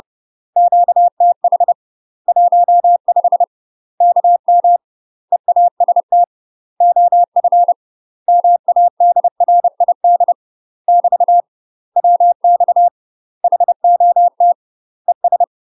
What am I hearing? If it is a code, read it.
Morse code, heard as a keyed tone, about 30 wpm, 700 Hz. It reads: QTH 15 KM EAST OF MADRID <BT> WX HOT ES
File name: UK MASTER QSOs TRAINING / 30WPM / UK MASTER QSOs 0029